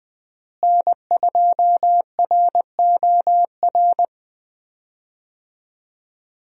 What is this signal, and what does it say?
Morse code, heard as a keyed tone, about 20 wpm, 700 Hz. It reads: N2ROR
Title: USA Call Signs - 5 Characters - Repeated 2x - 20wpm